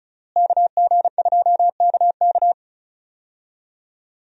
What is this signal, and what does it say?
Morse code, heard as a keyed tone, about 36 wpm, 700 Hz. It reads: KG2KK
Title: USA Call Signs - 5 Characters - Repeated 3x - 35wpm